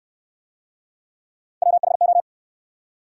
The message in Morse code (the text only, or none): WHO